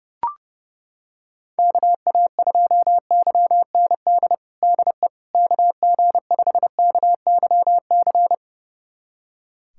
Morse code, heard as a keyed tone, about 32 wpm, 700 Hz. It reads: KA2YND DE KG5KYC
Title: USA Callsign DE Callsign - 6 Characters - 30wpm